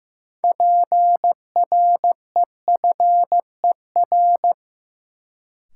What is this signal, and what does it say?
Morse code, heard as a keyed tone, about 15 wpm, 700 Hz. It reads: PREFER
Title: Sentences from Top 1100 Words - New Word Drill - 15wpm